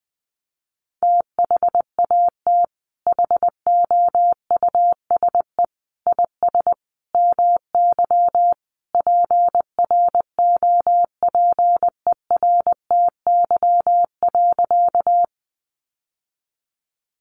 Morse code, heard as a keyed tone, about 20 wpm, 700 Hz. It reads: THAT HOUSE IS MY PROPERTY.
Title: Sentences from Top 1000 Words 20wpm